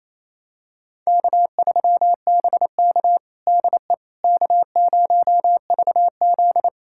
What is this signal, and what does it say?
Morse code, heard as a keyed tone, about 27 wpm, 700 Hz. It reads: K3BK DE K0VZ
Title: USA Callsign DE Callsign - 4 Characters - 28wpm